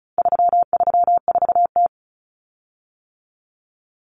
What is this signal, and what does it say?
Morse code, heard as a keyed tone, about 34 wpm, 700 Hz. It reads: 334T
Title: Common Cut Numbers - 4-Digit 35wpm